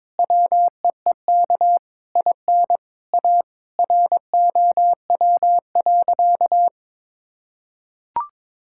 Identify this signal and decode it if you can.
Morse code, heard as a keyed tone, about 22 wpm, 700 Hz. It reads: WEEK IN A ROW.
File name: Sentences with 7-Word Phrases - 22wpm